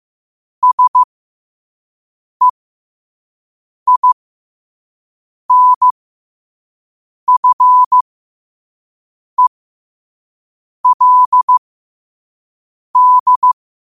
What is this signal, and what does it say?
Morse code, heard as a keyed tone, about 15 wpm, spaced as for 5 wpm, 1000 Hz. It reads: SEINFELD
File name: qotdfiles / 5wpm